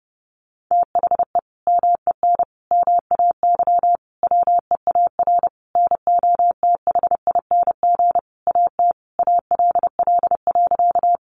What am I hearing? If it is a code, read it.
Morse code, heard as a keyed tone, about 30 wpm, 700 Hz. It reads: THE MEN MAY WEAR NOTHING AT ALL.
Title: Sentences from Top 1500 Words 30wpm